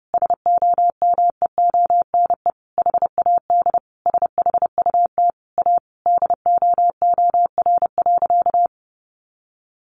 Morse code, heard as a keyed tone, about 30 wpm, 700 Hz. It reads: SOMEONE HAD SHUT A DOOR.
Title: A Sound Of Thunder 30wpm